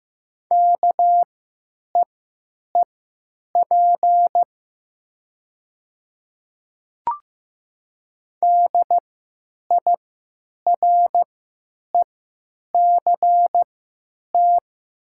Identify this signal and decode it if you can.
Morse code, heard as a keyed tone, about 15 wpm, spaced as for 9 wpm, 700 Hz. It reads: KEEP DIRECT
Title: 3x Character Spacing - Top 500 Words - 15wpm